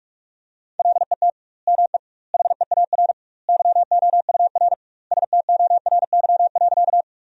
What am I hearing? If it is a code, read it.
Morse code, heard as a keyed tone, about 43 wpm, 700 Hz. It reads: LET ME HEAR YOUR STORY.
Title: 2x Word-Spacing - Sentences from Top 300 Words - 45wpm